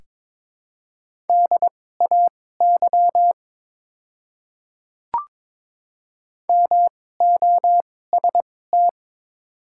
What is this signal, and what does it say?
Morse code, heard as a keyed tone, about 22 wpm, spaced as for 16 wpm, 700 Hz. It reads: DAY MOST
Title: 2x Character Spacing - Top 100 Words - 22wpm